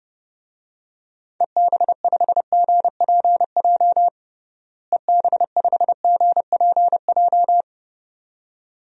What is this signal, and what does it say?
Morse code, heard as a keyed tone, about 32 wpm, 700 Hz. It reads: EB5GPJ EB5GPJ
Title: DXCC-100 Call Signs - 6 Characters - Repeated 2x - 30wpm